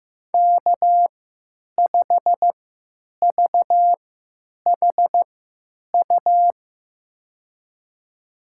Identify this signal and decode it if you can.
Morse code, heard as a keyed tone, about 15 wpm, spaced as for 9 wpm, 700 Hz. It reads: K5VHU
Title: USA Call Signs - 5 Characters - 3x Character Spacing - 15wpm